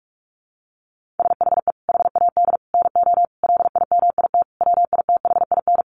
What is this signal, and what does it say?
Morse code, heard as a keyed tone, about 44 wpm, 700 Hz. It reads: SHE HAD NO LIMIT WITHIN
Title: Sentences from Top 1500 Words 45wpm